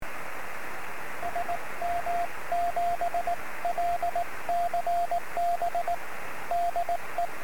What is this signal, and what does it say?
Morse code, heard as a keyed tone, about 18 wpm, 690 Hz, with radio noise. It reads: SM7LCB DE
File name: sm7lcb / 01-oh0jfp-23cm